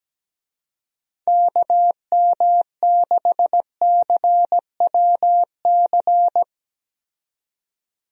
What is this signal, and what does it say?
Morse code, heard as a keyed tone, about 17 wpm, 700 Hz. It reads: KM6CWC
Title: USA Call Signs - 6 Characters - 17wpm